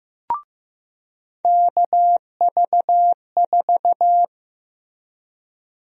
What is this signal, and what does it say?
Morse code, heard as a keyed tone, about 15 wpm, 700 Hz. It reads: KV4
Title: USA Call Sign Prefixes - Repeated 2x - 15wpm